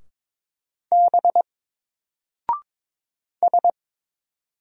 Morse code, heard as a keyed tone, about 22 wpm, 700 Hz. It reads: B S
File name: Single-Letter-Number-22wpm